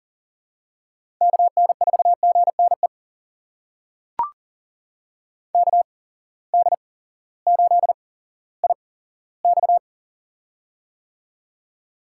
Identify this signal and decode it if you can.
Morse code, heard as a keyed tone, about 40 wpm, spaced as for 18 wpm, 700 Hz. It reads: KN4GNE KD8IX
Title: USA Call Signs - 8x Character Spacing - 40wpm